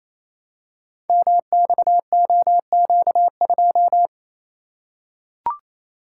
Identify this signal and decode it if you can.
Morse code, heard as a keyed tone, about 29 wpm, 700 Hz. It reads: MXOQ2
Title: Code Groups - 5 Letter:Numbers - 28wpm